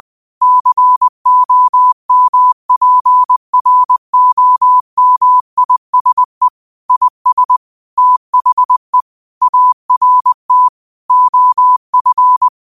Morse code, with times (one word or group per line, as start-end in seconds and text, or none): 0.4-6.5 COMPROMISE
6.9-7.6 IS
8.0-9.0 THE
9.4-10.7 ART
11.1-12.5 OF